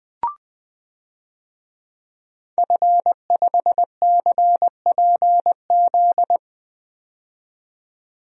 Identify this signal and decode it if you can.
Morse code, heard as a keyed tone, about 21 wpm, 700 Hz. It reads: F5CPZ
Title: DXCC-100 Call Signs - 5 Characters - 20wpm